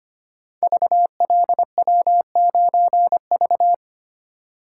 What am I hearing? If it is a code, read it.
Morse code, heard as a keyed tone, about 25 wpm, 700 Hz. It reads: VLW9V
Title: Code Groups - 5 Letter:Numbers - 25wpm